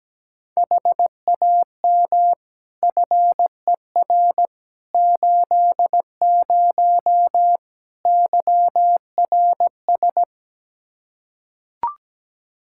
Morse code, heard as a keyed tone, about 17 wpm, 700 Hz. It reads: HAM FER 80 YRS
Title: Practice QSO - Exchange 3 - More Info - 17wpm